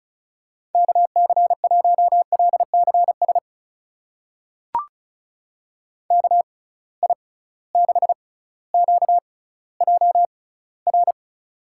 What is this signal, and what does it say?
Morse code, heard as a keyed tone, about 36 wpm, spaced as for 18 wpm, 700 Hz. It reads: KC1LCS KI6QJR
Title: USA Call Signs - 6 Characters - 6x Character Spacing - 35wpm